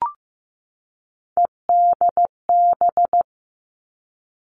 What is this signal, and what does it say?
Morse code, heard as a keyed tone, about 15 wpm, 700 Hz. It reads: EDB
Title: Three Challenging Letter-Number Combinations 15wpm